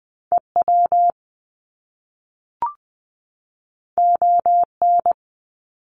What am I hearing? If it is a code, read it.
Morse code, heard as a keyed tone, about 20 wpm, 700 Hz. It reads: EW ON
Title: Top Two Letter Combinations in Words 20wpm